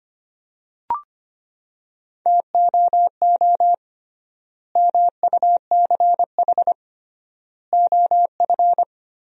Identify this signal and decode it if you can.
Morse code, heard as a keyed tone, about 25 wpm, 700 Hz. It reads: TOO MUCH OF
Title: 3x Word-Spacing - Sentences from Top 200 Words - 25wpm